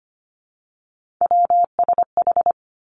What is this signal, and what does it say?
Morse code, heard as a keyed tone, about 25 wpm, 700 Hz. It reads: WSH
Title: Three Challenging Letter-Number Combinations 25wpm